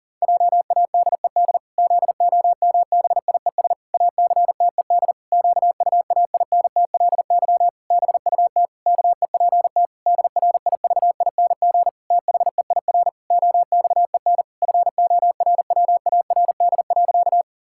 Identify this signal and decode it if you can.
Morse code, heard as a keyed tone, about 40 wpm, 700 Hz. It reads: JADED ZOMBIES ACTED QUAINTLY BUT KEPT DRIVING THEIR OXEN FORWARD.
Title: Pangram Challenge v2 - 40wpm